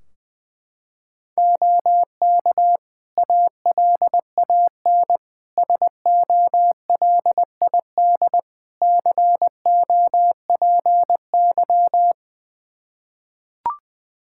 Morse code, heard as a keyed tone, about 20 wpm, 700 Hz. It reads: OK ALAN SOLID COPY
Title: Practice QSO - Exchange 3 - More Info - 20wpm